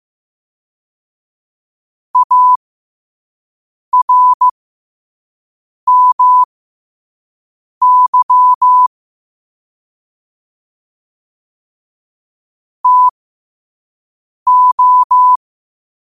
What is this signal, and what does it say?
Morse code, heard as a keyed tone, about 15 wpm, spaced as for 5 wpm, 1000 Hz. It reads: ARMY TO
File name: qotdfiles / 5wpm